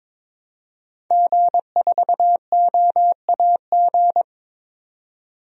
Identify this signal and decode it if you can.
Morse code, heard as a keyed tone, about 22 wpm, 700 Hz. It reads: G4OAG
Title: DXCC-100 Call Signs - 5 Characters - 22wpm